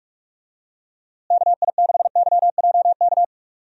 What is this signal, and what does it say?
Morse code, heard as a keyed tone, about 41 wpm, 700 Hz. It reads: KI6YJK
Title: USA Call Signs - 6 Characters - Repeated 3x - 45wpm